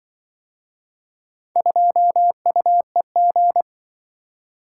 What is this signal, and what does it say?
Morse code, heard as a keyed tone, about 24 wpm, 700 Hz. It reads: 2UEG
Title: Speed-Racing - Four Letter-Number Combinations - 20wpm